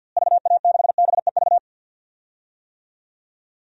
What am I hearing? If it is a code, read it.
Morse code, heard as a keyed tone, about 52 wpm, 700 Hz. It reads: VA6BEV
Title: DXCC-100 Call Signs - 6 Characters - Repeated 3x - 50wpm